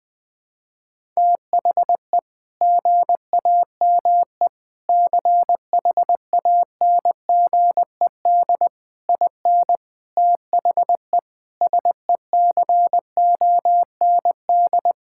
Morse code, heard as a keyed tone, about 21 wpm, 700 Hz. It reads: THE GAME CHANGED IN THE SECOND
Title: Sentences with 5-Word Phrases - Spoken Beginning - 20wpm